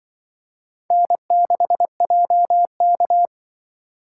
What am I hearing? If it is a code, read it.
Morse code, heard as a keyed tone, about 25 wpm, 700 Hz. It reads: N6JK
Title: Speed Racing - USA 1x2 Call Signs 20wpm